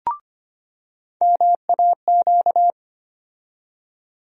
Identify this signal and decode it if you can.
Morse code, heard as a keyed tone, about 25 wpm, 700 Hz. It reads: MAQ
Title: USA Call Sign Postfixes - 3 Letters 25wpm